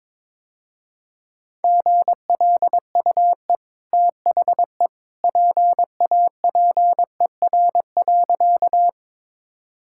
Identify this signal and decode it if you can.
Morse code, heard as a keyed tone, about 22 wpm, 690 Hz. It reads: GLUE THE PAPER.
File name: practice / 22wpm / Easy Sentences with Hints - 22wpm